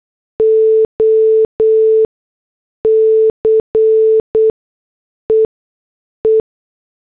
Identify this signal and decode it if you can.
Morse code, heard as a keyed tone, about 8 wpm, spaced as for 6 wpm, 440 Hz. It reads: OC EE